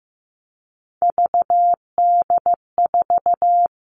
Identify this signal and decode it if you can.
Morse code, heard as a keyed tone, about 15 wpm, 700 Hz. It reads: VD4
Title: All Cut Numbers - 3-Digit 15wpm